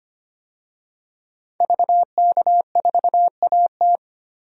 Morse code, heard as a keyed tone, about 25 wpm, 700 Hz. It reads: VK4AT
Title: DXCC-100 Call Signs - 5 Characters - 25wpm